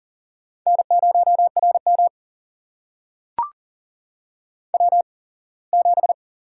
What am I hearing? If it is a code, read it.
Morse code, heard as a keyed tone, about 40 wpm, spaced as for 24 wpm, 700 Hz. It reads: N0RM W7
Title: USA Call Signs - 4 Characters - 8x Character Spacing - 40wpm